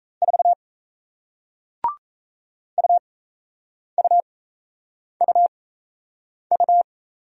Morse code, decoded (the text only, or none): 4 UUUU